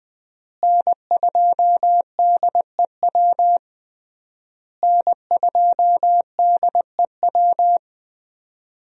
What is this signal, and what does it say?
Morse code, heard as a keyed tone, about 20 wpm, 700 Hz. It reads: N2DEW N2DEW
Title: USA Call Signs - 5 Characters - Repeated 3x - 20wpm